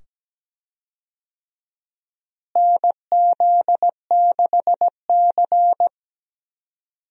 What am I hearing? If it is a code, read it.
Morse code, heard as a keyed tone, about 17 wpm, 700 Hz. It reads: NZ6C
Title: USA Call Signs - 4 Characters - Repeated 2x - 17wpm